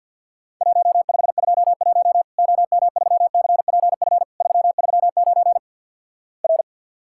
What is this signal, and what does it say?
Morse code, heard as a keyed tone, about 53 wpm, 700 Hz. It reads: 1521 OM3CPF 339 R
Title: Worked All Europe DX Contest - QTC Exchange - 50wpm